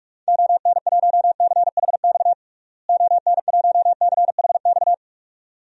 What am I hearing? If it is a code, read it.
Morse code, heard as a keyed tone, about 46 wpm, 700 Hz. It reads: ON1CHX ON1CHX
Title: DXCC-100 Call Signs - 6 Characters - Repeated 2x - 45wpm